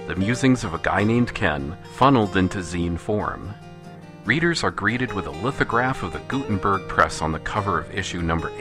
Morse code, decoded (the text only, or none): none